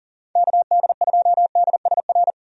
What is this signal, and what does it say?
Morse code, heard as a keyed tone, about 38 wpm, 700 Hz. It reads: KD2DSR
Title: USA Call Signs - 6 Characters - 40wpm